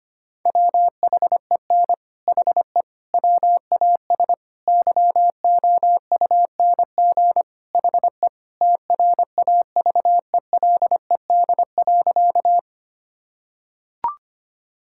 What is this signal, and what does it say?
Morse code, heard as a keyed tone, about 25 wpm, 700 Hz. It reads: WHEN HE WAS YOUNG HE TRAVELED.
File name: practice / 25wpm / Sentences with 3-Word Phrases - 25wpm